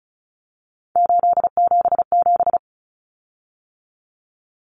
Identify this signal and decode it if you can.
Morse code, heard as a keyed tone, about 34 wpm, 700 Hz. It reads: 877
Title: Common Cut Numbers - 3-Digit 35wpm